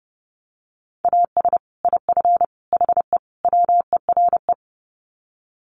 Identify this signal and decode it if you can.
Morse code, heard as a keyed tone, about 30 wpm, 700 Hz. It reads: AS IF HE WERE
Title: Sets of 4 Words 30wpm